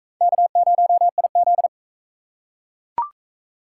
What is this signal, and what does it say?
Morse code, heard as a keyed tone, about 44 wpm, 700 Hz. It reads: K0IZ